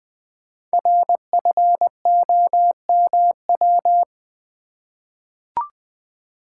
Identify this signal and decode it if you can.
Morse code, heard as a keyed tone, about 21 wpm, 700 Hz. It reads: RFOMW
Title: Code Groups - 5 Letters - 20wpm